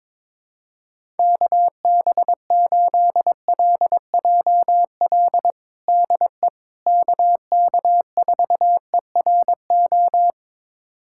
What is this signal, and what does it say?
Morse code, heard as a keyed tone, about 22 wpm, 700 Hz. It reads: KB8LJL DE KK4ERO
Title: USA Callsign DE Callsign 22wpm